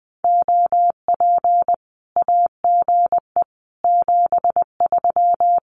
Morse code, heard as a keyed tone, about 20 wpm, 700 Hz. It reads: OP AGE 73
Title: Common QSO Phrases 20wpm